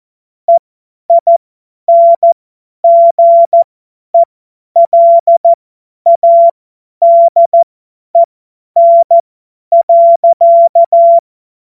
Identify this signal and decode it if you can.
Morse code, heard as a keyed tone, about 14 wpm, spaced as for 10 wpm, 680 Hz. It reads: EINGELADEN.
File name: Morsekurs / Witze / Witz 25. Januar 2022, 50 F70